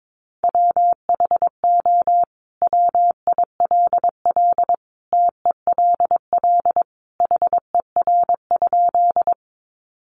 Morse code, heard as a keyed tone, about 22 wpm, 700 Hz. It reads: WHO WILL TELL HER?
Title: Sentences from Top 200 Words 22wpm